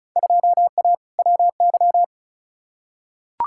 Morse code, heard as a keyed tone, about 36 wpm, 700 Hz. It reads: 2A WY
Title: Field Day Exchanges - 35wpm